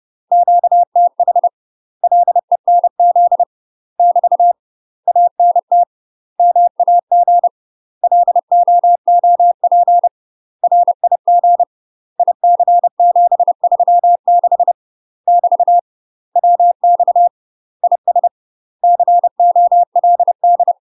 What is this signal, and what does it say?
Morse code, heard as a keyed tone, about 30 wpm, 700 Hz. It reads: QTH LENZ <BT> ANT MAG LOOP RIG IC736 <BT> WX IS COLD
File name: UK MASTER QSOs TRAINING / 30WPM / UK MASTER QSOs 0038